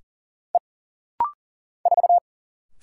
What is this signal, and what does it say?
Morse code, heard as a keyed tone, about 40 wpm, 700 Hz. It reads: E 4
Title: Single Letter-Number Alt - Rapid-Fire 40wpm